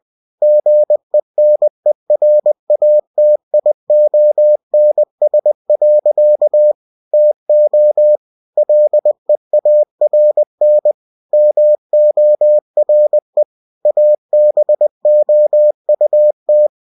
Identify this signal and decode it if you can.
Morse code, heard as a keyed tone, about 20 wpm, 600 Hz. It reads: GENERATIONS. TO LEARN MORE ABOUT
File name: cw-books / winnie-the-pooh / winnie-20wpm-page-0120